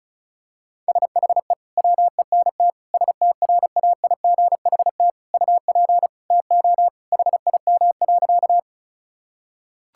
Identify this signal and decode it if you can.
Morse code, heard as a keyed tone, about 34 wpm, 700 Hz. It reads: SHE WENT STRAIGHT UP TO HIM.
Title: Sentences from Top 1000 Words - Review Part 6 - 35wpm